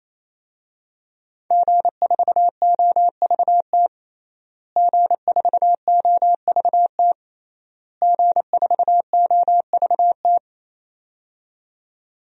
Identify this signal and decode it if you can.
Morse code, heard as a keyed tone, about 28 wpm, 700 Hz. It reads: G4OVT G4OVT G4OVT
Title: DXCC-100 Call Signs - 5 Characters - Repeated 3x - 28wpm